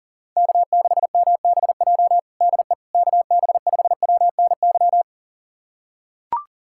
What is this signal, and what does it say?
Morse code, heard as a keyed tone, about 40 wpm, 700 Hz. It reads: K6MBJ DE KB5WNY
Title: USA Callsign DE Callsign 40wpm